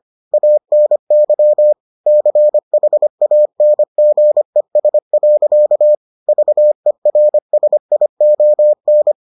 Morse code, heard as a keyed tone, about 25 wpm, 600 Hz. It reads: ANY CHANGES. VERSION